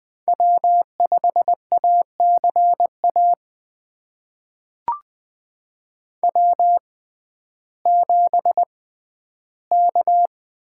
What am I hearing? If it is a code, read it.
Morse code, heard as a keyed tone, about 20 wpm, spaced as for 12 wpm, 700 Hz. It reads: W5ACA W7K
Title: USA Call Signs - 5 Characters - 6x Character Spacing - 20wpm